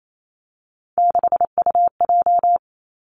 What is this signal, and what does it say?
Morse code, heard as a keyed tone, about 28 wpm, 700 Hz. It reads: 6UJ
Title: Three Challenging Letter-Number Combinations 28wpm